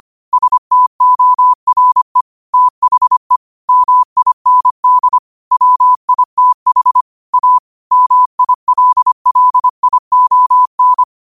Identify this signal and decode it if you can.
Morse code, heard as a keyed tone, about 25 wpm, 1000 Hz. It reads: STORE THE MIND WITH A MILLION